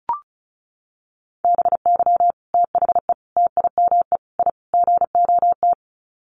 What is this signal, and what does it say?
Morse code, heard as a keyed tone, about 34 wpm, 700 Hz. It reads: BY THE TIME I GOT
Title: Sets of 5 Words Encore2 35wpm